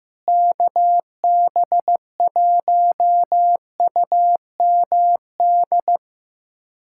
Morse code, heard as a keyed tone, about 15 wpm, 700 Hz. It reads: KB1UMD
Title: USA Callsign DE Callsign - 6 Characters - 15wpm